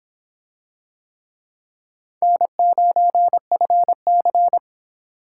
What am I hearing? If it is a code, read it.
Morse code, heard as a keyed tone, about 24 wpm, 700 Hz. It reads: N9FC